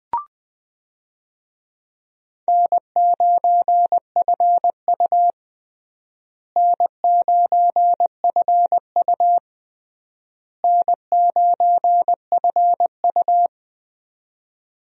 Morse code, heard as a keyed tone, about 20 wpm, 700 Hz. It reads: N9FU N9FU N9FU
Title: USA Call Signs - 4 Characters - Repeated 3x - 20wpm